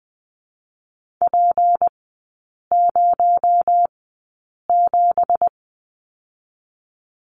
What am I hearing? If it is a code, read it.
Morse code, heard as a keyed tone, about 20 wpm, 700 Hz. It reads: P 0 7
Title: 218 Lesson 35e - Number 0 - Three Characters 20wpm